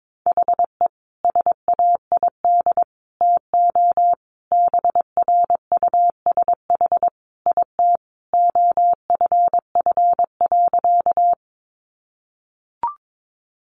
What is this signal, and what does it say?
Morse code, heard as a keyed tone, about 22 wpm, 700 Hz. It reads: HE SAID TO BRUSH IT OFF.
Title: Sentences from Top 1100 Words 22wpm